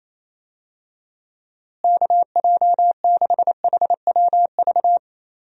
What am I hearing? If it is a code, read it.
Morse code, heard as a keyed tone, about 28 wpm, 700 Hz. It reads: KJ6HWV
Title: USA Call Signs - 6 Characters - Repeated 2x - 28wpm